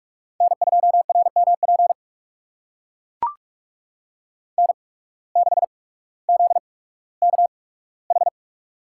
Morse code, heard as a keyed tone, about 44 wpm, spaced as for 19 wpm, 700 Hz. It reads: N2RMP N6ZKH
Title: USA Call Signs - 5 Characters - 8x Character Spacing - 45wpm